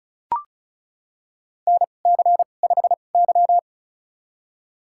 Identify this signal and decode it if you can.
Morse code, heard as a keyed tone, about 34 wpm, spaced as for 25 wpm, 700 Hz. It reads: NC5Y
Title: USA Call Signs - 2x Character Spacing - 35wpm